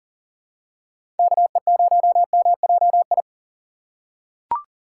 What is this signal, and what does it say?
Morse code, heard as a keyed tone, about 40 wpm, 700 Hz. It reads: KE0MJI
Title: USA Call Signs - 6 Characters - 3x Character Spacing - 40wpm